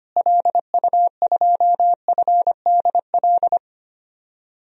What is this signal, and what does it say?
Morse code, heard as a keyed tone, about 25 wpm, 700 Hz. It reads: LU2FDL
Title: DXCC-100 Call Signs - 6 Characters - Repeated 3x - 25wpm